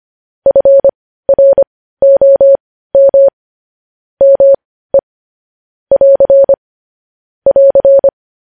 Morse code, heard as a keyed tone, about 25 wpm, spaced as for 15 wpm, 550 Hz. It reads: FROM ME <AR> <AR>
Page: news-headlines-in-morse-code-at-15-wpm